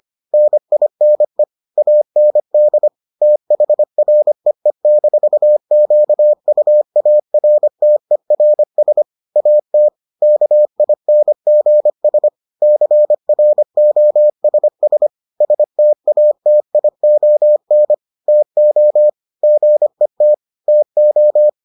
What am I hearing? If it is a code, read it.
Morse code, heard as a keyed tone, about 25 wpm, 600 Hz. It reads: NINE AND THREE-QUARTERS AT KINGS CROSS STATION TO GET TO